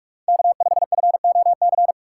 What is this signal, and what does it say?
Morse code, heard as a keyed tone, about 44 wpm, 700 Hz. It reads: K 5 F O C
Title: USA Call Signs - 5 Characters - 6x Character Spacing - 45wpm